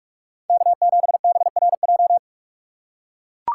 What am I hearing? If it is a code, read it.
Morse code, heard as a keyed tone, about 44 wpm, 700 Hz. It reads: K7BRJ
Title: USA Call Signs - 2x Character Spacing - 45wpm